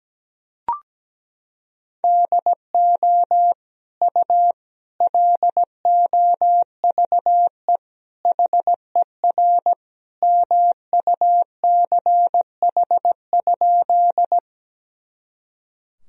Morse code, heard as a keyed tone, about 17 wpm, 700 Hz. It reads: DO U LOVE HER MUCH?
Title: Sentences from Top 400 Words and Abbreviations 17wpm